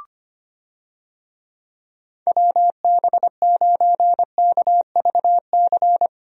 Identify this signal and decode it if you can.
Morse code, heard as a keyed tone, about 25 wpm, 700 Hz. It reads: WB9KVC
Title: USA Call Signs - 6 Characters - Repeated 3x - 25wpm